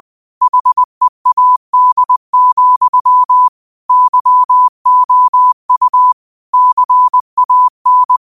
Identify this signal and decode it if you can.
Morse code, heard as a keyed tone, about 20 wpm, 1000 Hz. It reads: HEAD, YOU CAN
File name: qotdfiles / 20wpm